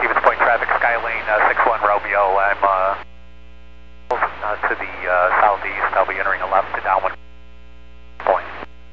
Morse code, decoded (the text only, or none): none